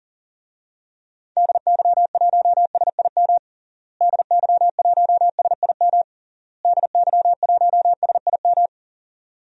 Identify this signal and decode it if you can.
Morse code, heard as a keyed tone, about 40 wpm, 700 Hz. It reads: DY1SIM DY1SIM DY1SIM
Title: DXCC-100 Call Signs - 6 Characters - Repeated 3x - 40wpm